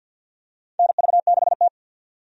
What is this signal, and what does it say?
Morse code, heard as a keyed tone, about 51 wpm, 700 Hz. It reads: NV6T
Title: USA Call Signs - 4 Characters - 50wpm